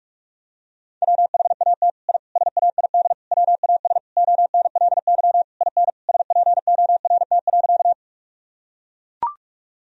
Morse code, heard as a keyed tone, about 41 wpm, 700 Hz. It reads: WHAT I SAID WAS ONLY IN SPORT.
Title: Sentences from Top 1000 Words - Review Part 4 - 45wpm